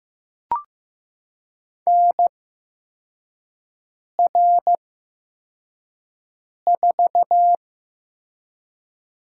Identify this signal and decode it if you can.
Morse code, heard as a keyed tone, about 15 wpm, spaced as for 4 wpm, 700 Hz. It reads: NR4
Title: USA Call Signs - 4 Characters - 8x Character Spacing - 15wpm